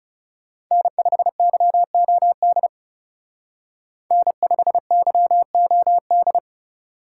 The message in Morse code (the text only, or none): N5YOD N5YOD